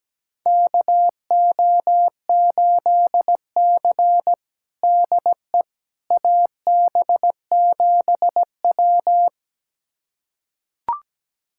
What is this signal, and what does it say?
Morse code, heard as a keyed tone, about 17 wpm, 700 Hz. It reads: KO8C DE AB7W
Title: USA Callsign DE Callsign - 4 Characters - 17wpm